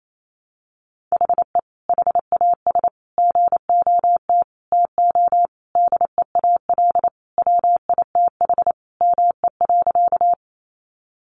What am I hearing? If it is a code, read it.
Morse code, heard as a keyed tone, about 28 wpm, 700 Hz. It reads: HE HAS GOT TO DEAL WITH ME.
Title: Sentences from Top 800 Words 28wpm